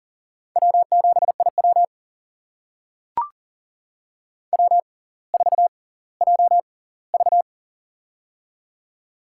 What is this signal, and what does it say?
Morse code, heard as a keyed tone, about 39 wpm, spaced as for 20 wpm, 690 Hz. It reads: W7IW W4JV
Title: USA Call Signs - 4 Characters - 6x Character Spacing - 40wpm